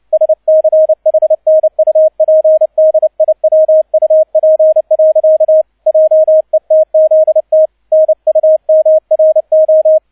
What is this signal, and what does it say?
Morse code, heard as a keyed tone, about 29 wpm, 630 Hz, with radio noise. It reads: SCHNUPDIWUP. JETZT NUMRO
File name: 002 Zweiter Streich 30wpm